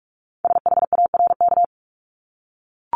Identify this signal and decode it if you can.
Morse code, heard as a keyed tone, about 46 wpm, 700 Hz. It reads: SHARK
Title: Multi-letter Phonemes 45wpm